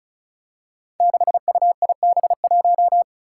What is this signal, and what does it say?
Morse code, heard as a keyed tone, about 36 wpm, 700 Hz. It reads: 6UIB1
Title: Five Challenging Letter-Number Combinations 35wpm